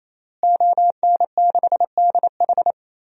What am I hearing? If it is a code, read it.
Morse code, heard as a keyed tone, about 28 wpm, 700 Hz. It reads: ON6DH
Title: DXCC-100 Call Signs - 5 Characters - Repeated 3x - 28wpm